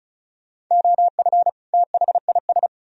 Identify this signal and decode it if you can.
Morse code, heard as a keyed tone, about 46 wpm, 700 Hz. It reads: OF THIS